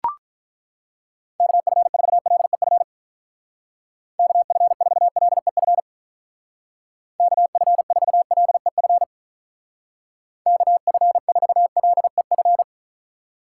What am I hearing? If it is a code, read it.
Morse code, heard as a keyed tone, about 42 wpm, 700 Hz. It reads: KF4LEF KF4LEF KF4LEF KF4LEF
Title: Speed Racing - USA 2x3 Call Signs 35wpm